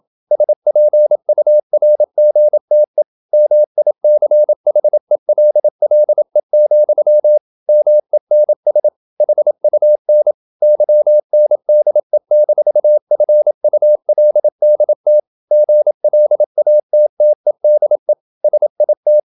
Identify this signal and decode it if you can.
Morse code, heard as a keyed tone, about 27 wpm, 600 Hz. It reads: SPURGTE MICHELLE, MENS HUN YNDE-FULDT GLATTEDE SIT
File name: Ferie QRQ 27wpm